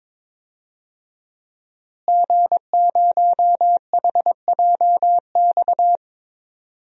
Morse code, heard as a keyed tone, about 22 wpm, 700 Hz. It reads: G0HJX
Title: DXCC-100 Call Signs - 5 Characters - Repeated 2x - 22wpm